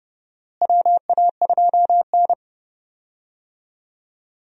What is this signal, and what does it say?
Morse code, heard as a keyed tone, about 32 wpm, 700 Hz. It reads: WA2N
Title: USA Call Signs - 4 Characters - Repeated 2x - 30wpm